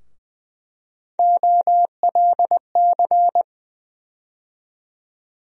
Morse code, heard as a keyed tone, about 20 wpm, 700 Hz. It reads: OLC
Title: USA Call Sign Postfixes - 3 Letters 20wpm